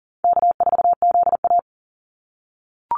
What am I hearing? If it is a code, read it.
Morse code, heard as a keyed tone, about 38 wpm, 700 Hz. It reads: K4ZA
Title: Easy Call Signs - 40wpm